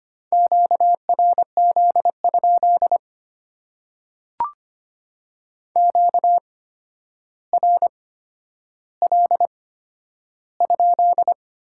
Morse code, heard as a keyed tone, about 25 wpm, spaced as for 11 wpm, 700 Hz. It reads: QRZ? QRL?
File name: Top 90 QSO Elements - 8x Character Spacing - 25wpm